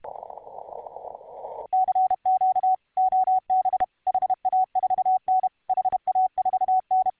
Morse code, heard as a keyed tone, about 32 wpm, 750 Hz. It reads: CQ OB HA4N HA4N